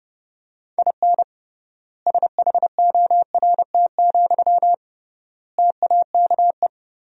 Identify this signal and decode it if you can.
Morse code, heard as a keyed tone, about 29 wpm, 700 Hz. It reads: IN SHORT, TAKE
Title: 3x Word-Spacing - Sentences from Top 400 Words - 30wpm